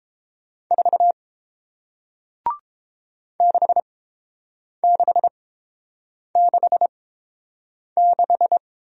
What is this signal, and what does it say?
Morse code, heard as a keyed tone, about 30 wpm, spaced as for 7 wpm, 700 Hz. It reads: 4 6666